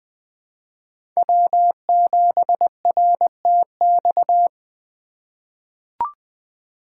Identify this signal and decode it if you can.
Morse code, heard as a keyed tone, about 20 wpm, 700 Hz. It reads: W7RTX
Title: USA Call Signs - 3x Character Spacing - 20wpm